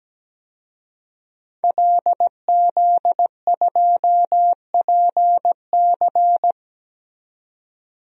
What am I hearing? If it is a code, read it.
Morse code, heard as a keyed tone, about 17 wpm, 700 Hz. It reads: LZ2PC
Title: DXCC-100 Call Signs - 5 Characters - 17wpm